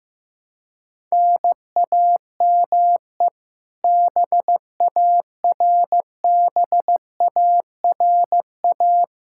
Morse code, heard as a keyed tone, about 15 wpm, 700 Hz. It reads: NAME BARBARA
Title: Practice QSO - Exchange 1 - Beginning - 15wpm